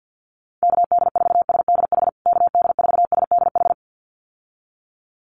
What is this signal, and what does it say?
Morse code, heard as a keyed tone, about 50 wpm, 700 Hz. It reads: KD4SDH KD4SDH
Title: USA Call Signs Repeated 2x 50wpm